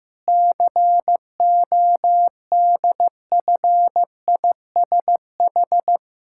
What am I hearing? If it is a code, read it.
Morse code, heard as a keyed tone, about 16 wpm, 700 Hz. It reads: CODFISH
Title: Pangram Challenge v2 - 15wpm